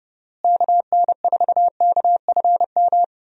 Morse code, heard as a keyed tone, about 29 wpm, 700 Hz. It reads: KN4KFM
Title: USA Call Signs - 6 Characters - Repeated 3x - 30wpm